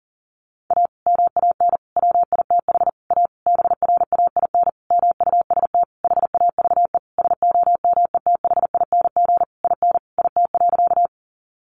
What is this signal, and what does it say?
Morse code, heard as a keyed tone, about 40 wpm, 700 Hz. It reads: A MAN WITH A BRAIN MUST HAVE SOMETHING IN IT.
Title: Sentences from Top 1100 Words 40wpm